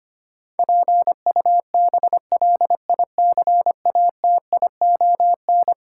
Morse code, heard as a keyed tone, about 25 wpm, 700 Hz. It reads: PUBLICATION
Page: morse-code-ninja-practice-25wpm